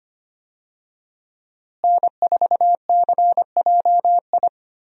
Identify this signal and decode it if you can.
Morse code, heard as a keyed tone, about 25 wpm, 700 Hz. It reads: N4CJI
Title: USA Call Signs - 5 Characters - Repeated 2x - 25wpm